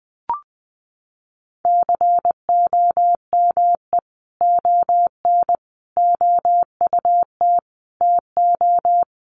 Morse code, heard as a keyed tone, about 20 wpm, 700 Hz. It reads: COME ON OUT TO
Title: Sentences from Top 1500 Words 20wpm